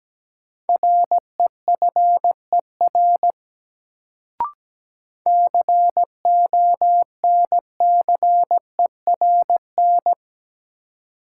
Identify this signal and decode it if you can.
Morse code, heard as a keyed tone, about 17 wpm, 700 Hz. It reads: REFER CONCERN
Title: Sentences from Top 1000 Words - New Word Drill - 17wpm